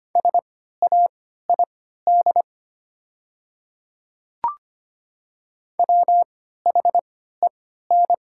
Morse code, heard as a keyed tone, about 25 wpm, spaced as for 14 wpm, 700 Hz. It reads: SAID WHEN
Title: 3x Character Spacing - Top 100 Words - 25wpm